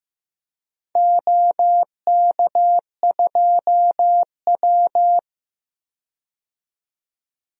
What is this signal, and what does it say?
Morse code, heard as a keyed tone, about 15 wpm, 700 Hz. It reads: OK2W
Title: DXCC-100 Call Signs - 4 Characters - 15wpm